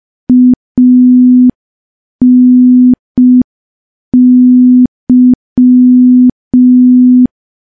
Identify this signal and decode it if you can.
Morse code, heard as a keyed tone, about 5 wpm, 250 Hz. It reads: ANY
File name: morse / morse-training-05WPM